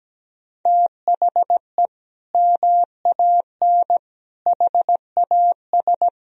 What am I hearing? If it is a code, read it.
Morse code, heard as a keyed tone, about 17 wpm, 700 Hz. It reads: THE MAN HAS
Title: Sentences from Top 1000 Words - Review Part 3 - 17wpm